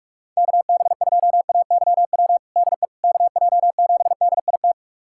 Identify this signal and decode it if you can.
Morse code, heard as a keyed tone, about 47 wpm, 700 Hz. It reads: KB2AYW DE KJ7DIT
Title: USA Callsign DE Callsign - 6 Characters - 45wpm